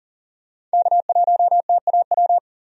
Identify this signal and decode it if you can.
Morse code, heard as a keyed tone, about 40 wpm, 700 Hz. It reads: K1TAW
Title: USA Call Signs - 5 Characters - 40wpm